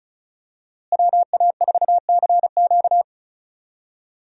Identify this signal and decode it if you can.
Morse code, heard as a keyed tone, about 34 wpm, 700 Hz. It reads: WA4CQ
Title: USA Call Signs - 5 Characters - Repeated 2x - 35wpm